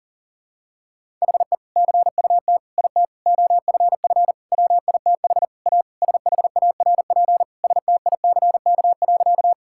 Morse code, heard as a keyed tone, about 40 wpm, 700 Hz. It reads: HE CUT IT OFF WITH A SHARP STICK.
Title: Sentences from Top 1600 Words 40wpm